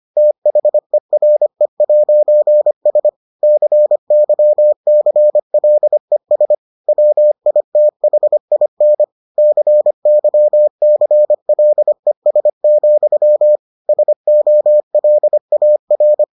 Morse code, heard as a keyed tone, about 25 wpm, 600 Hz. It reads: THERE'S CYCLES WITHIN CYCLES, SOLAR